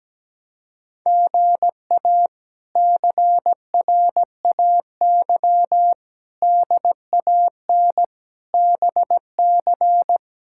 Morse code, heard as a keyed tone, about 17 wpm, 700 Hz. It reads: GA CRAY DAN BC
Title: K1USN SST QSO Practice - 17wpm